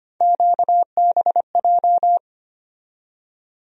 Morse code, heard as a keyed tone, about 25 wpm, 700 Hz. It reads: QBJ
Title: USA Call Sign Postfixes - 3 Letters 25wpm